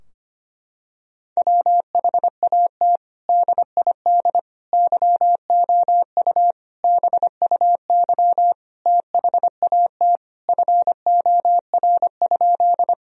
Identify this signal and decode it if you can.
Morse code, heard as a keyed tone, about 25 wpm, 700 Hz. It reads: WHAT DID YOU BUY THAT FOR?
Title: Sentences from Top 1000 Words - Review Part 4 - 25wpm